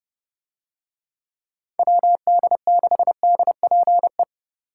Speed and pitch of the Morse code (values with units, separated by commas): 30 wpm, 700 Hz